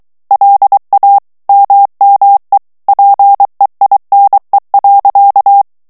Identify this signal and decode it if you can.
Morse code, heard as a keyed tone, about 23 wpm, 800 Hz. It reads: LA MME PEINE.
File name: morse mp3 / FRANCAIS TEXTE CLAIR 2 VITESSE 22 MOTS MINUTES